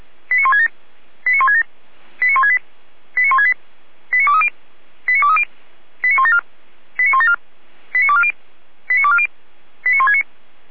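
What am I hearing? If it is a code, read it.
Five-tone selective call (ZVEI), heard as quick runs of five tones, tones of about 70 ms. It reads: 89167 89167 89168 89168 89230 89230 89174 89174 89240 89240 89169
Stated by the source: ZVEI five-tone call 89120 received by SDR.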